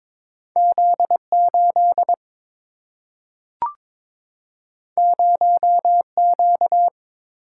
Morse code, heard as a keyed tone, about 22 wpm, 700 Hz. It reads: Z8 0Q
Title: Single-Letter Single-Number 22wpm